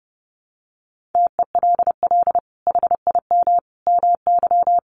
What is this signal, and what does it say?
Morse code, heard as a keyed tone, about 30 wpm, 700 Hz. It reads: TELL HIM MY
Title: Sentences from Top 1200 Words 30wpm